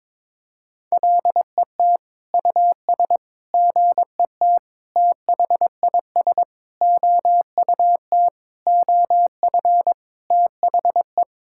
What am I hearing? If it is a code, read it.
Morse code, heard as a keyed tone, about 22 wpm, 700 Hz. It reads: LET US GET THIS OUT OF THE
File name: Sentences with 7-Word Phrases - 22wpm